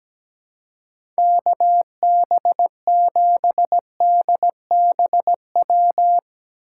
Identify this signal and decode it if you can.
Morse code, heard as a keyed tone, about 17 wpm, 700 Hz. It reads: KB7DBW
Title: USA Call Signs - 6 Characters - 8x Character Spacing - 17wpm